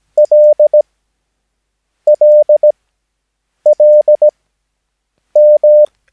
Morse code, heard as a keyed tone, about 17 wpm, spaced as for 7 wpm, 600 Hz. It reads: LLLM
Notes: Övningsmaterial med alfabetet A till Ö samt siffror 0 till 9 enligt International/Continental Morse Code.